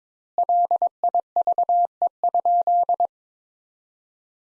Morse code, heard as a keyed tone, about 22 wpm, 700 Hz. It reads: LIVE?
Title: Sentences Top 200 Words - 22wpm